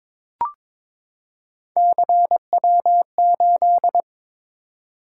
Morse code, heard as a keyed tone, about 22 wpm, 700 Hz. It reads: CW8
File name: practice / 22wpm / Three Letter-Number Combinations - 22wpm